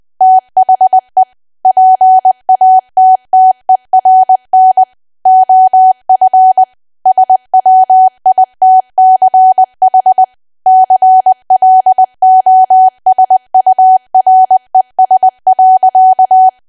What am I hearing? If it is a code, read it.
Morse code, heard as a keyed tone, about 20 wpm, 750 Hz. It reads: THE PATTERN OF SWITCH CLOSURES.
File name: cwa / W1AW-Level3 / 20WPM